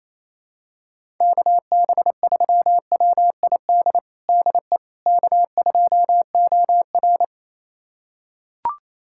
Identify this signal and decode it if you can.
Morse code, heard as a keyed tone, about 28 wpm, 700 Hz. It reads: KB3WID DE K2OR
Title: USA Callsign DE Callsign 28wpm